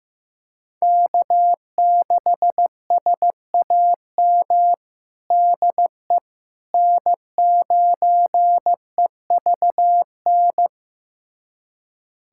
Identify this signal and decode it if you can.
Morse code, heard as a keyed tone, about 15 wpm, 700 Hz. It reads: K6SAM DE N9EVN
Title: USA Callsign DE Callsign - 5 Characters - 15wpm